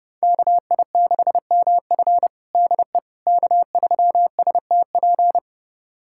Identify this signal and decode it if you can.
Morse code, heard as a keyed tone, about 30 wpm, 700 Hz. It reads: KI6MF DE K3STP
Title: USA Callsign DE Callsign - 5 Characters - 30wpm